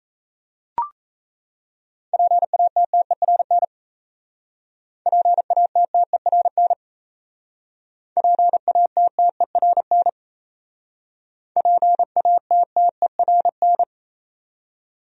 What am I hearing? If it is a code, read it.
Morse code, heard as a keyed tone, about 36 wpm, 700 Hz. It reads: PATTERN PATTERN PATTERN PATTERN